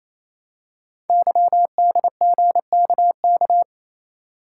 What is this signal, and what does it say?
Morse code, heard as a keyed tone, about 27 wpm, 700 Hz. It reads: YDGKK
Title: Code Groups - 5 Letters - 28wpm